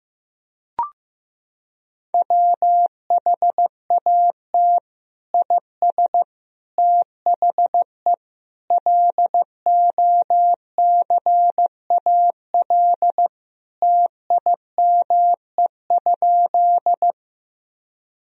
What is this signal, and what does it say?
Morse code, heard as a keyed tone, about 15 wpm, 700 Hz. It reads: WHAT IS THE LOCAL TIME?
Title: Sentences from Top 1000 Words - Review Part 3 - 15wpm